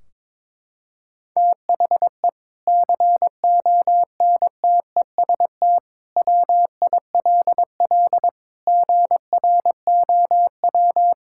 Morse code, heard as a keyed tone, about 22 wpm, 700 Hz. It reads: THE CONTEST WILL GROW
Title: Sentences from Top 1600 Words 22wpm